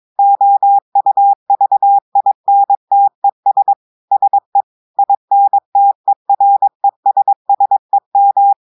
Morse code, heard as a keyed tone, about 23 wpm, 800 Hz. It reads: OUVINTES SE INTERESSEM
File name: audios / portugais / portugais texte en clair 2 vitesse 22 mots minutes